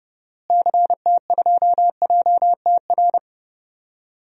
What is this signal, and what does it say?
Morse code, heard as a keyed tone, about 31 wpm, 700 Hz. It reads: CT2JTR
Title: DXCC-100 Call Signs - 6 Characters - Repeated 3x - 30wpm